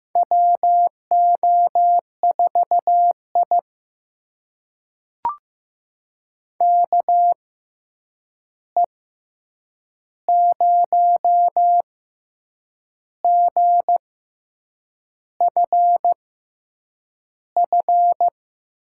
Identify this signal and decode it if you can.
Morse code, heard as a keyed tone, about 15 wpm, spaced as for 7 wpm, 700 Hz. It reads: WO4I KE0GFF
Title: USA Call Signs - 6x Character Spacing - 15wpm